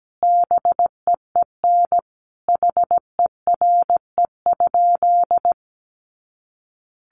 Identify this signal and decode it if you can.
Morse code, heard as a keyed tone, about 17 wpm, 700 Hz. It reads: BEEN HERE?
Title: Sentences from Top 200 Words 17wpm